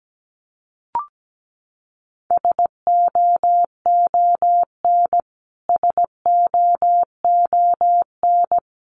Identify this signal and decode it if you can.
Morse code, heard as a keyed tone, about 17 wpm, 700 Hz. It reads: SOON SOON
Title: Top 500 Words Repeated 17wpm